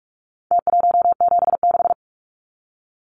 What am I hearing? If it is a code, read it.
Morse code, heard as a keyed tone, about 44 wpm, 700 Hz. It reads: T176
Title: Numbers - Contest Style - 4-Digit 45wpm